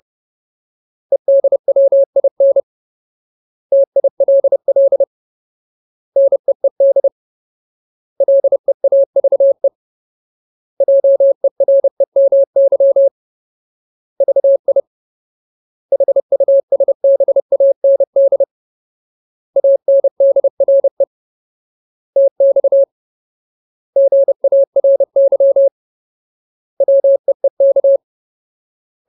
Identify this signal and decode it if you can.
Morse code, heard as a keyed tone, about 30 wpm, 550 Hz. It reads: EDWIN TILL NEED LEAVE JEREMY VI HUSBAND ANDRE TX GARY WEEK
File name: Ultimate Word List / 30WPM Ultimate